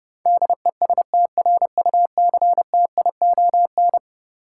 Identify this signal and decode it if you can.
Morse code, heard as a keyed tone, about 32 wpm, 700 Hz. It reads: DESTRUCTION
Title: Sets of 5 Words Encore - New Word Drill - 30wpm